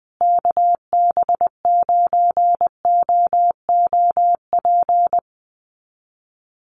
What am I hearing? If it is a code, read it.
Morse code, heard as a keyed tone, about 20 wpm, 700 Hz. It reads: KB9OOP
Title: USA Call Signs Repeated 3x 20wpm